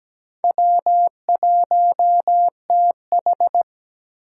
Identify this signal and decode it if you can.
Morse code, heard as a keyed tone, about 20 wpm, 700 Hz. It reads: W1TH
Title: Speed Racing - USA 1x2 Call Signs 17wpm